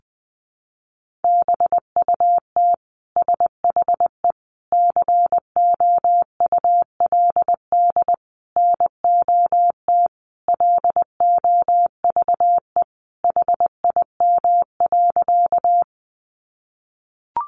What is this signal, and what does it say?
Morse code, heard as a keyed tone, about 20 wpm, 700 Hz. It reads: BUT SHE COULD NOT LOVE HIM.
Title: Sentences from Top 600 Words 20wpm